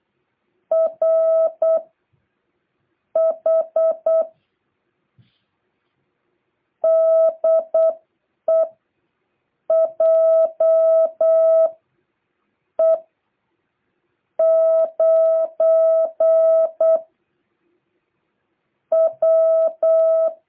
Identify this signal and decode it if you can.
Morse code, heard as a keyed tone, about 8 wpm, spaced as for 5 wpm, 660 Hz. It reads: RH DE JE9W